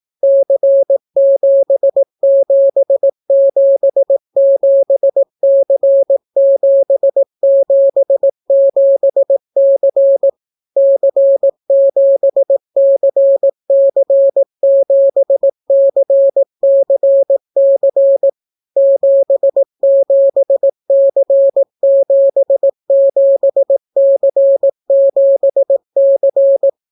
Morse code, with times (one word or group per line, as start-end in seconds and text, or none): 0.2-10.3 C7777C777C
10.8-18.3 C7CC7CCC
18.8-26.7 77C77C7C